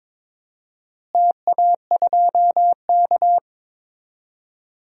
Morse code, heard as a keyed tone, about 22 wpm, 700 Hz. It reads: TA2K
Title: DXCC-100 Call Signs - 4 Characters - Repeated 2x - 22wpm